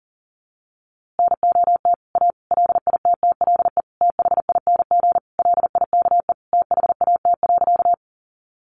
Morse code, heard as a keyed tone, about 40 wpm, 700 Hz. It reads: NOT A LITTLE THING LIKE THAT.
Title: A Sound Of Thunder 40wpm